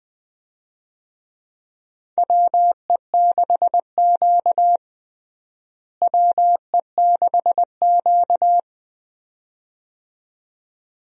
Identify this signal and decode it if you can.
Morse code, heard as a keyed tone, about 20 wpm, 700 Hz. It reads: WE6Q WE6Q
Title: USA Call Signs - 4 Characters - Repeated 2x - 20wpm